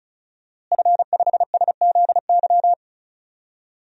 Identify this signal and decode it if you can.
Morse code, heard as a keyed tone, about 36 wpm, 700 Hz. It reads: F5SZY
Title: DXCC-100 Call Signs - 5 Characters - Repeated 3x - 35wpm